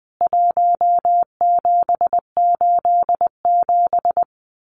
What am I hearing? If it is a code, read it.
Morse code, heard as a keyed tone, about 20 wpm, 700 Hz. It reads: 1787
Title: CWT Exchange 20wpm